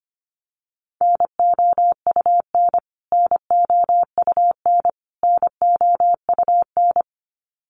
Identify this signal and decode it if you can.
Morse code, heard as a keyed tone, about 25 wpm, 700 Hz. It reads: NOUN NOUN NOUN
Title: Top 500 Words Repeated 25wpm